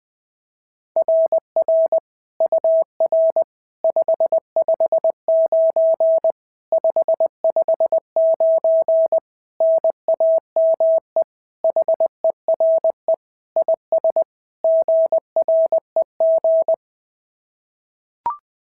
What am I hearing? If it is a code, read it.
Morse code, heard as a keyed tone, about 21 wpm, 690 Hz. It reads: RR UR 559 559 NAME HERE IS GREG
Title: SOTA QSO Practice 20wpm